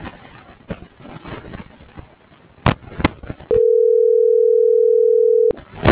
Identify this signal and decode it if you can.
Ringback tone (North America), 440 + 480 Hz.